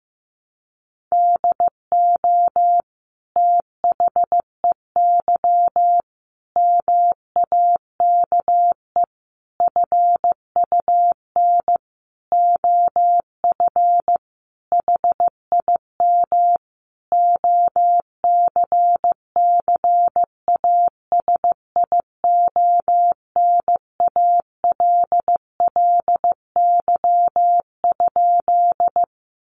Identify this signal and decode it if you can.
Morse code, heard as a keyed tone, about 15 wpm, 700 Hz. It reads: DO THEY MAKE FUN OF HIM OCCASIONALLY?
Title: Sentences from Top 1300 Words 15wpm